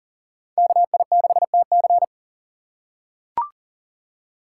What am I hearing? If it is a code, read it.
Morse code, heard as a keyed tone, about 40 wpm, 700 Hz. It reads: KI6TC
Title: USA Call Signs - 4x Character Spacing - 40wpm